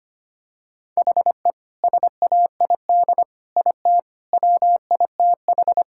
Morse code, heard as a keyed tone, about 25 wpm, 700 Hz. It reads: HE SAID IT WITH
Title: Sentences from Top 1600 Words 25wpm